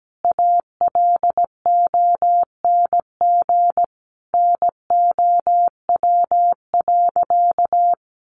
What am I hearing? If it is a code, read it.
Morse code, heard as a keyed tone, about 17 wpm, 700 Hz. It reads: ALONG NOW.
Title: Sentences from Top 700 Words 17wpm